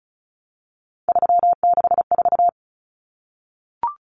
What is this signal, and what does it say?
Morse code, heard as a keyed tone, about 34 wpm, 700 Hz. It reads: 364
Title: Common Cut Numbers - 3-Digit 35wpm